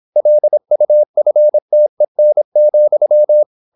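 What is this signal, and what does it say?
Morse code, heard as a keyed tone, about 26 wpm, 600 Hz. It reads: LUFTEN,
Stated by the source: High-speed (QRQ) Morse at 26 wpm.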